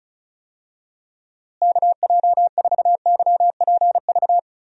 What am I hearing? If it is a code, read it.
Morse code, heard as a keyed tone, about 36 wpm, 700 Hz. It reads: KJ4YPV
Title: USA Call Signs - 6 Characters - Repeated 3x - 35wpm